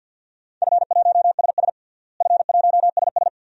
Morse code, heard as a keyed tone, about 53 wpm, 700 Hz. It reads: F1SS F1SS
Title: DXCC-100 Call Signs - 4 Characters - Repeated 2x - 50wpm